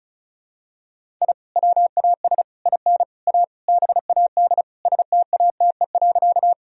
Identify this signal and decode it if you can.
Morse code, heard as a keyed tone, about 36 wpm, 700 Hz. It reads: I WAS IN A BAD STATE.
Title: Sentences from Top 1000 Words - Review Part 2 - 35wpm